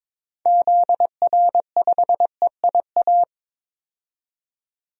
Morse code, heard as a keyed tone, about 22 wpm, 700 Hz. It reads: ZR5EIA
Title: DXCC-100 Call Signs - 6 Characters - 22wpm